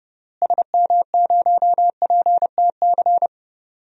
Morse code, heard as a keyed tone, about 32 wpm, 700 Hz. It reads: SM0PTC
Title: DXCC-100 Call Signs - 6 Characters - Repeated 3x - 30wpm